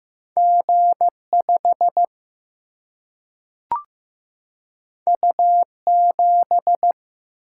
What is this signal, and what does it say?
Morse code, heard as a keyed tone, about 15 wpm, 700 Hz. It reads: G 5 U 7
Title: Single-Letter Single-Number 15wpm